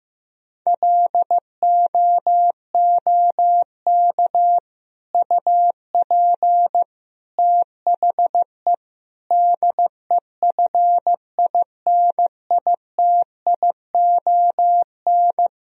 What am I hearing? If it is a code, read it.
Morse code, heard as a keyed tone, about 16 wpm, 700 Hz. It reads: LOOK UP THE DEFINITION
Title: Sentences from Top 1000 Words - Review Part 4 - 15wpm